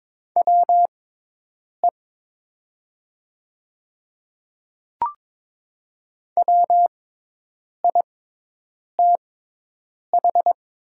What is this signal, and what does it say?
Morse code, heard as a keyed tone, about 22 wpm, spaced as for 8 wpm, 700 Hz. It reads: WE WITH
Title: 6x Character Spacing - Top 100 Words - 22wpm